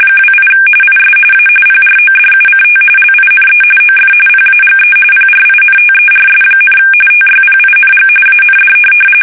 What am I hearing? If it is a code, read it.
RTTY, about 45.45 baud. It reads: 28 IS A PRODUCT LINE OF PAGE PRINTERS, TYPING AND
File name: fsk850 / teletype-corp